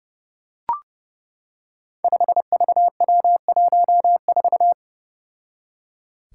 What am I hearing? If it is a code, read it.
Morse code, heard as a keyed tone, about 32 wpm, 700 Hz. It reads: 5VW14
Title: Five Challenging Letter-Number Combinations 30wpm